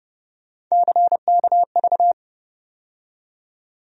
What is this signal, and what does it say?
Morse code, heard as a keyed tone, about 30 wpm, 700 Hz. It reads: CKV
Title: USA Call Sign Postfixes - 3 Letters 30wpm